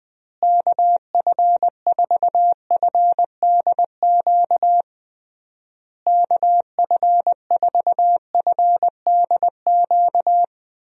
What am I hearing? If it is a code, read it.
Morse code, heard as a keyed tone, about 20 wpm, 700 Hz. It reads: KF4FDQ KF4FDQ
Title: USA Call Signs - 6 Characters - Repeated 3x - 20wpm